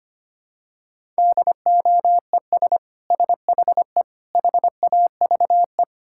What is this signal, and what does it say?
Morse code, heard as a keyed tone, about 25 wpm, 700 Hz. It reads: DOES SHE HAVE
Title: Sentences from Top 1000 Words - Review Part 6 - 25wpm